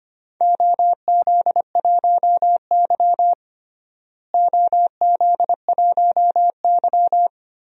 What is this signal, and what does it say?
Morse code, heard as a keyed tone, about 25 wpm, 700 Hz. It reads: OZ1Y OZ1Y
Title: DXCC-100 Call Signs - 4 Characters - Repeated 3x - 25wpm